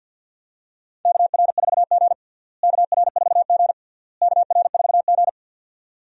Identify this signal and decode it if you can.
Morse code, heard as a keyed tone, about 47 wpm, 700 Hz. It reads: KR4G KR4G KR4G
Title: USA Call Signs - 4 Characters - Repeated 3x - 50wpm